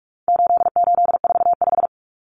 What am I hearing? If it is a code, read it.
Morse code, heard as a keyed tone, about 44 wpm, 700 Hz. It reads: 8845
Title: Numbers - 4-Digit 45wpm